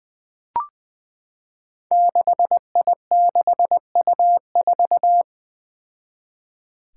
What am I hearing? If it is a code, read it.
Morse code, heard as a keyed tone, about 20 wpm, 700 Hz. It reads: 6I6U4
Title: Five Challenging Letter-Number Combinations 20wpm